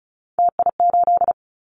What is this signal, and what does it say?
Morse code, heard as a keyed tone, about 34 wpm, 700 Hz. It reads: TI8
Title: Callsign Prefixes for Top 100 DXCC - 35wpm